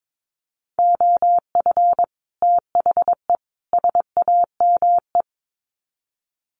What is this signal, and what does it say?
Morse code, heard as a keyed tone, about 22 wpm, 700 Hz. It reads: OF THE SAME
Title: Sets of 3 Words 22wpm